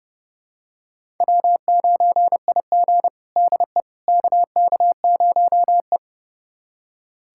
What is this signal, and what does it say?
Morse code, heard as a keyed tone, about 30 wpm, 700 Hz. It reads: W9IG DE KK0E
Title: USA Callsign DE Callsign - 4 Characters - 30wpm